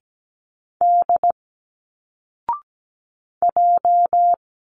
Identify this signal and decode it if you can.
Morse code, heard as a keyed tone, about 17 wpm, 700 Hz. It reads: D J
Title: Single Letters 17wpm